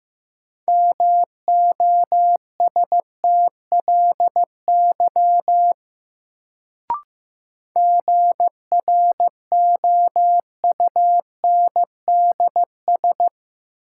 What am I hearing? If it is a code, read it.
Morse code, heard as a keyed tone, about 15 wpm, 700 Hz. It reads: MOSTLY GROUNDS
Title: Sentences from Top 1300 Words - New Word Drill - 15wpm